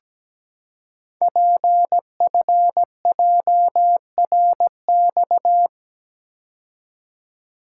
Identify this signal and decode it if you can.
Morse code, heard as a keyed tone, about 17 wpm, 700 Hz. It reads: PFJRX
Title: Code Groups - 5 Letters - 17wpm